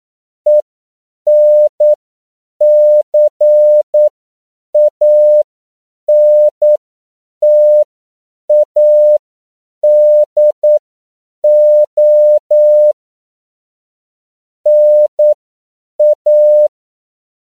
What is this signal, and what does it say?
Morse code, heard as a keyed tone, about 9 wpm, spaced as for 7 wpm, 600 Hz. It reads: ENCANTADO NA
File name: CW / 10PPM